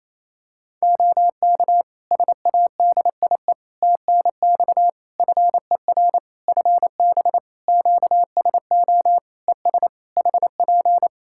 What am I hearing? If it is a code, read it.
Morse code, heard as a keyed tone, about 28 wpm, 700 Hz. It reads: OK SADIE TNX FER FB QSO ES HP
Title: Practice QSO - Exchange 4 - Ending - 28wpm